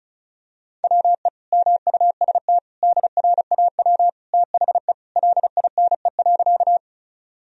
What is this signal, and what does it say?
Morse code, heard as a keyed tone, about 36 wpm, 700 Hz. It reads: WE MUST DRAW THE LINE.
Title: Sentences from Top 1000 Words - Review Part 1 - 35wpm